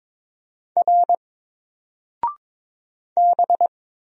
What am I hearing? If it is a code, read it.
Morse code, heard as a keyed tone, about 22 wpm, 700 Hz. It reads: R B
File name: Single-Letter-Number-22wpm